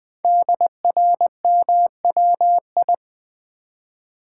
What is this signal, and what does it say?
Morse code, heard as a keyed tone, about 20 wpm, 700 Hz. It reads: DRMWI
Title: Code Groups - 5 Letters - 20wpm